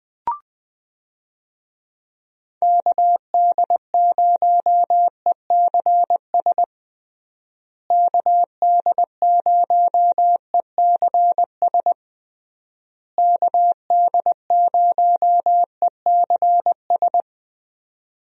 Morse code, heard as a keyed tone, about 21 wpm, 700 Hz. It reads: KD0ECS KD0ECS KD0ECS
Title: USA Call Signs - 6 Characters - Repeated 3x - 20wpm